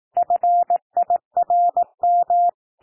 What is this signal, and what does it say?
Morse code, heard as a keyed tone, about 18 wpm, 700 Hz. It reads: FIRM
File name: TEXT PASSAGE 18 wpm